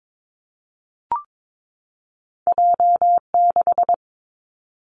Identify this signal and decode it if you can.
Morse code, heard as a keyed tone, about 22 wpm, 700 Hz. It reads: J6
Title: Two Challenging Letter-Number Combinations 22wpm